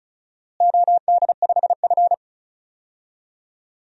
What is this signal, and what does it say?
Morse code, heard as a keyed tone, about 36 wpm, 700 Hz. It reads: OD5F
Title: DXCC-100 Call Signs - 4 Characters - Repeated 3x - 35wpm